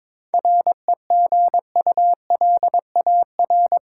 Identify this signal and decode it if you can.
Morse code, heard as a keyed tone, about 22 wpm, 700 Hz. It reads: REGULAR
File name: Difficult Sentences with Hints - 22wpm